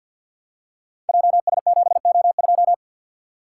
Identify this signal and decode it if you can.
Morse code, heard as a keyed tone, about 52 wpm, 700 Hz. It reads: JS7O2
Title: Code Groups - 5 Letter:Numbers - 50wpm